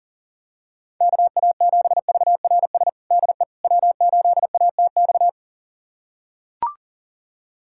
Morse code, heard as a keyed tone, about 37 wpm, 700 Hz. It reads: KA7VRS DE W8ATX
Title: USA Callsign DE Callsign 40wpm